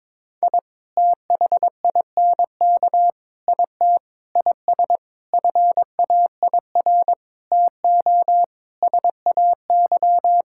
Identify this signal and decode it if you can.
Morse code, heard as a keyed tone, about 22 wpm, 700 Hz. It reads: I THINK IT IS FAIR TO SAY
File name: practice / 22wpm / Sets of 8 Words - 22wpm